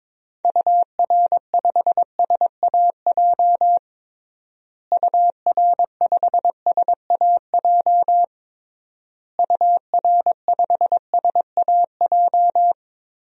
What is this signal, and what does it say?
Morse code, heard as a keyed tone, about 22 wpm, 700 Hz. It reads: UR5SAJ UR5SAJ UR5SAJ
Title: DXCC-100 Call Signs - 6 Characters - Repeated 3x - 22wpm